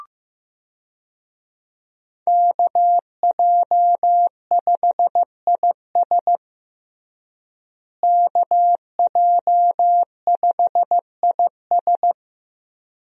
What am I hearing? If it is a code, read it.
Morse code, heard as a keyed tone, about 15 wpm, 700 Hz. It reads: KJ5IS KJ5IS
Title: USA Call Signs - 5 Characters - Repeated 2x - 15wpm